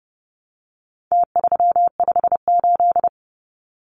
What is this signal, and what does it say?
Morse code, heard as a keyed tone, about 30 wpm, 700 Hz. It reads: T358
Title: Numbers - Contest Style - 4-Digit 30wpm